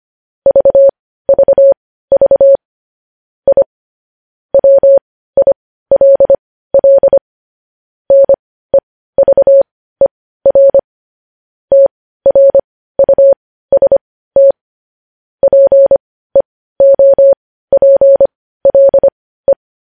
Morse code, heard as a keyed tone, about 25 wpm, spaced as for 15 wpm, 550 Hz. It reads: VVV I WILL NEVER TRUST PEOPLE
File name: podcast / News headlines-1x-25x15